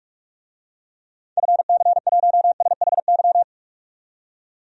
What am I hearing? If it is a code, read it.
Morse code, heard as a keyed tone, about 46 wpm, 700 Hz. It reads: FC1SHY
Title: DXCC-100 Call Signs - 6 Characters - 45wpm